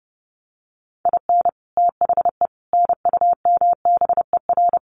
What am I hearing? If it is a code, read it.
Morse code, heard as a keyed tone, about 30 wpm, 700 Hz. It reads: IN THE NUMBER
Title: Sets of 4 Words 30wpm